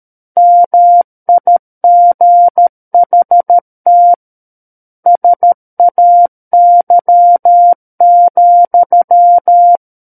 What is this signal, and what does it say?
Morse code, heard as a keyed tone, about 13 wpm, 700 Hz. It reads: MIGHT SAY,
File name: Book One / 04-Chapter 4 - 13 WPM